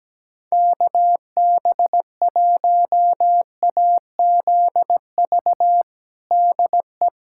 Morse code, heard as a keyed tone, about 17 wpm, 700 Hz. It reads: KB1AZV DE
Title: USA Callsign DE Callsign - 6 Characters - 17wpm